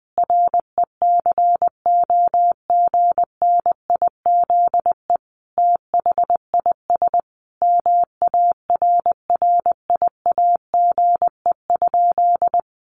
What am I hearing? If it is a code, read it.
Morse code, heard as a keyed tone, about 20 wpm, 700 Hz. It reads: RECOGNIZE THIS MARRIAGE?
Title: Sentences from Top 1400 Words 20wpm